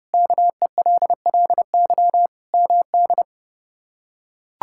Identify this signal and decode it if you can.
Morse code, heard as a keyed tone, about 30 wpm, 700 Hz. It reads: KELLY MD
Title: K1USN SST QSO Practice - 30wpm